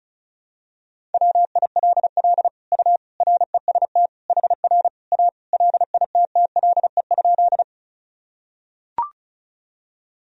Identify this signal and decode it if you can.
Morse code, heard as a keyed tone, about 36 wpm, 700 Hz. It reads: WILL U REST HR A LITTLE?
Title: Sentences from Top 500 Words and Abbreviations 35wpm